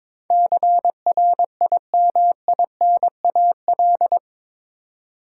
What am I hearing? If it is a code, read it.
Morse code, heard as a keyed tone, about 21 wpm, 700 Hz. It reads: CRIMINAL
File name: practice / 22wpm / Eight Letter Words - 22wpm